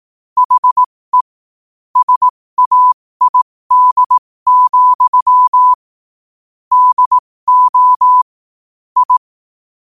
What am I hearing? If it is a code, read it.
Morse code, heard as a keyed tone, about 18 wpm, spaced as for 15 wpm, 1000 Hz. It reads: HE SAID, DO I
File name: qotdfiles / 15wpm